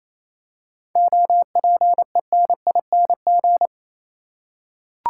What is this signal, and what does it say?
Morse code, heard as a keyed tone, about 28 wpm, 700 Hz. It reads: OPENING
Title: Seven Letter Words - 28wpm